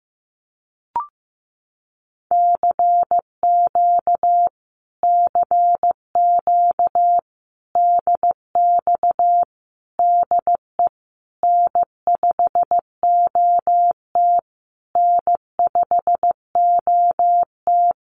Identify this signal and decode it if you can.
Morse code, heard as a keyed tone, about 15 wpm, 700 Hz. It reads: CQ CQ DX DE N5OT N5OT
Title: Common QSO Phrases Encore 15wpm